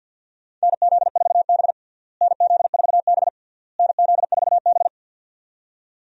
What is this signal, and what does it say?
Morse code, heard as a keyed tone, about 52 wpm, 700 Hz. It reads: NZ4B NZ4B NZ4B
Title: USA Call Signs - 4 Characters - Repeated 3x - 50wpm